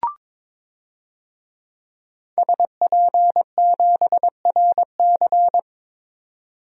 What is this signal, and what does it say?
Morse code, heard as a keyed tone, about 22 wpm, 700 Hz. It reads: SP7RC
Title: DXCC-100 Call Signs - 5 Characters - Repeated 2x - 22wpm